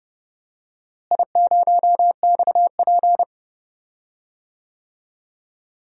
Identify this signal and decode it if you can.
Morse code, heard as a keyed tone, about 30 wpm, 700 Hz. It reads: I0XP
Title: DXCC-100 Call Signs - 4 Characters - Repeated 3x - 30wpm